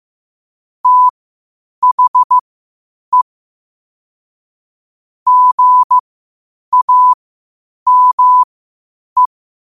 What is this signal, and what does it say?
Morse code, heard as a keyed tone, about 15 wpm, spaced as for 8 wpm, 1000 Hz. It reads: THE GAME
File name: qotdfiles / 8wpm